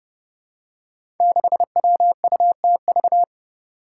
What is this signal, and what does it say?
Morse code, heard as a keyed tone, about 30 wpm, 700 Hz. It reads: 6WUTV
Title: Five Challenging Letter-Number Combinations 30wpm